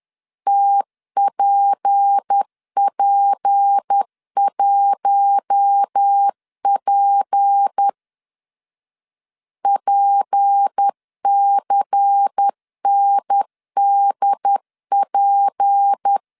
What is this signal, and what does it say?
Morse code, heard as a keyed tone, about 10 wpm, 800 Hz. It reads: TPP1P PCNDP
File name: LEZIONI / 6 PKX0 / 6 P 10wpm 2432